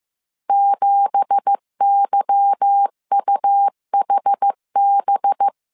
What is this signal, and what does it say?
Morse code, heard as a keyed tone, about 15 wpm, 800 Hz. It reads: 7YUHB
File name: CW / DET / 1 9 1500C S A 14wpm 3222